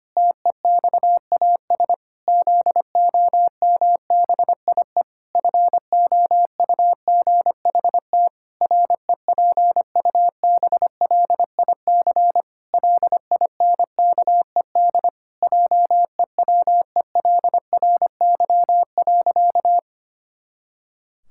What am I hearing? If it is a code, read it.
Morse code, heard as a keyed tone, about 25 wpm, 700 Hz. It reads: TEXAS ZOMBIE FOUGHT REPUBLIC LINKED JEWELRY.
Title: Pangram Challenge v2 - 25wpm